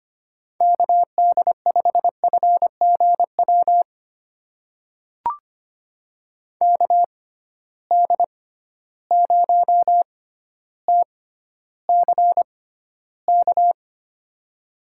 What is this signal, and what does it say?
Morse code, heard as a keyed tone, about 25 wpm, spaced as for 13 wpm, 700 Hz. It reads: KD5FGW KD0TCK
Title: USA Call Signs - 6 Characters - 6x Character Spacing - 25wpm